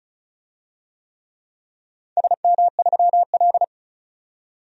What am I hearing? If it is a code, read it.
Morse code, heard as a keyed tone, about 34 wpm, 700 Hz. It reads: SM3L
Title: DXCC-100 Call Signs - 4 Characters - 35wpm